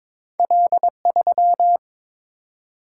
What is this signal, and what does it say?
Morse code, heard as a keyed tone, about 22 wpm, 700 Hz. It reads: L3
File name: Two Letter-Number Combinations - 22wpm